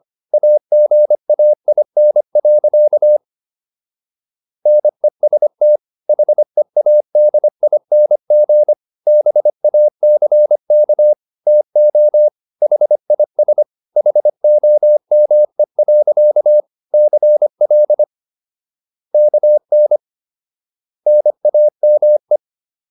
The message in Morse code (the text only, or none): AGAIN. NEST HEADING BACK TO HIS HOME. CL KN NAME